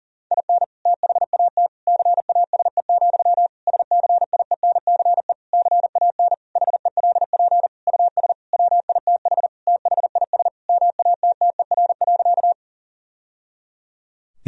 Morse code, heard as a keyed tone, about 40 wpm, 700 Hz. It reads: IN THAT CASE, SCIENCE CAN HELP US WITH THIS MATTER.
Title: Sentences from Top 1000 Words - Review Part 2 - 40wpm